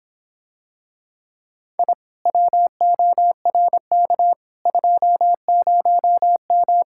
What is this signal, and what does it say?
Morse code, heard as a keyed tone, about 26 wpm, 700 Hz. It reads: I WORK 20M